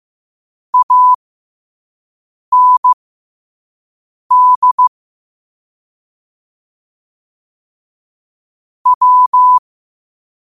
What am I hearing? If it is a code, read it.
Morse code, heard as a keyed tone, about 15 wpm, spaced as for 5 wpm, 1000 Hz. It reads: AND W
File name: qotdfiles / 5wpm